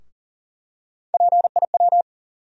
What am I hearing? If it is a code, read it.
Morse code, heard as a keyed tone, about 40 wpm, 700 Hz. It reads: PIW
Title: USA Call Sign Postfixes - 3 Letters 40wpm